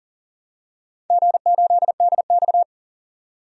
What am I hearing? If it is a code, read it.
Morse code, heard as a keyed tone, about 44 wpm, 690 Hz. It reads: G8DX
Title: DXCC-100 Call Signs - 4 Characters - 40wpm